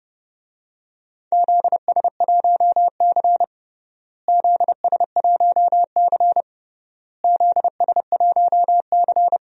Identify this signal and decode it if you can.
Morse code, heard as a keyed tone, about 30 wpm, 700 Hz. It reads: ZS1C ZS1C ZS1C
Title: DXCC-100 Call Signs - 4 Characters - Repeated 3x - 30wpm